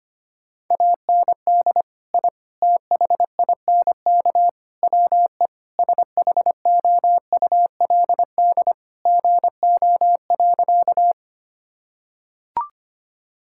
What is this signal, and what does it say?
Morse code, heard as a keyed tone, about 25 wpm, 700 Hz. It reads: AND I THINK WE SHOULD GO.
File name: practice / 25wpm / Sentences with 4-Word Phrases - 25wpm